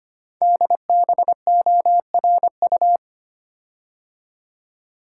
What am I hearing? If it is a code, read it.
Morse code, heard as a keyed tone, about 25 wpm, 700 Hz. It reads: DBORU
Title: Code Groups - 5 Letters - 25wpm